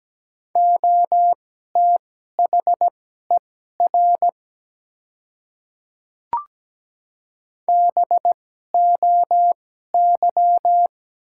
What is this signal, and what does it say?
Morse code, heard as a keyed tone, about 17 wpm, spaced as for 12 wpm, 700 Hz. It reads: OTHER BOY
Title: 2x Character Spacing - Top 200 Words - 17wpm